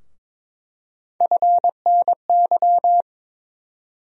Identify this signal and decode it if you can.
Morse code, heard as a keyed tone, about 21 wpm, 700 Hz. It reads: FNY
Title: USA Call Sign Postfixes - 3 Letters 22wpm